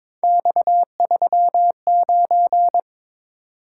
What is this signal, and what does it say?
Morse code, heard as a keyed tone, about 26 wpm, 700 Hz. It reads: X39
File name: Speed-Racing - Three Letter-Number Combinations - 22wpm